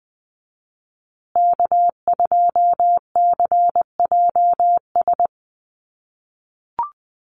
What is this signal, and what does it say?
Morse code, heard as a keyed tone, about 20 wpm, 700 Hz. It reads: K2CJS
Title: USA Call Signs Repeated 3x 20wpm